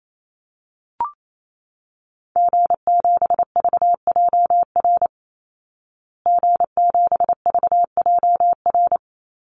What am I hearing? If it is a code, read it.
Morse code, heard as a keyed tone, about 28 wpm, 700 Hz. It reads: G7VJR G7VJR
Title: Call Signs Repeated 2x - 28wpm